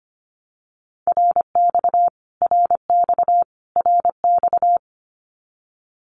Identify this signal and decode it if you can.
Morse code, heard as a keyed tone, about 25 wpm, 700 Hz. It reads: RX RX RX
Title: Abbreviations Repeated 3x - 25wpm